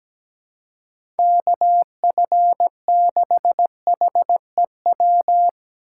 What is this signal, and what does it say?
Morse code, heard as a keyed tone, about 17 wpm, 700 Hz. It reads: KF6HEW
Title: USA Call Signs - 6x Character Spacing - 17wpm